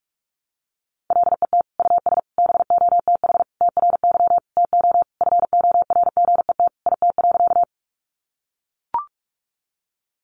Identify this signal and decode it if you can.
Morse code, heard as a keyed tone, about 44 wpm, 700 Hz. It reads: LET US BOTH TRY TO FORGET IT.
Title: Sentences from Top 1200 Words 45wpm